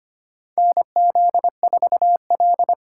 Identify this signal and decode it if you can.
Morse code, heard as a keyed tone, about 25 wpm, 700 Hz. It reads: NZ4L
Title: USA Call Signs - 4 Characters - 3x Character Spacing - 25wpm